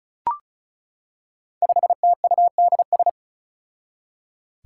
Morse code, heard as a keyed tone, about 34 wpm, 700 Hz. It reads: 5TUDS
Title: Five Challenging Letter-Number Combinations 35wpm